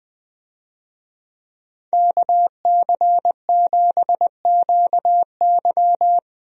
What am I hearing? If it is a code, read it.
Morse code, heard as a keyed tone, about 20 wpm, 700 Hz. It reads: KC7QY
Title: USA Call Signs - 5 Characters - Repeated 2x - 20wpm